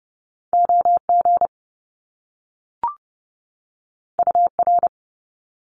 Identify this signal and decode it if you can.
Morse code, heard as a keyed tone, about 30 wpm, 700 Hz. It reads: OG UR
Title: Top Two Letter Combinations in Words 30wpm